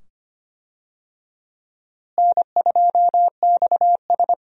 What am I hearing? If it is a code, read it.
Morse code, heard as a keyed tone, about 25 wpm, 700 Hz. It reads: N2XS
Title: USA Call Signs - 4 Characters - Repeated 2x - 25wpm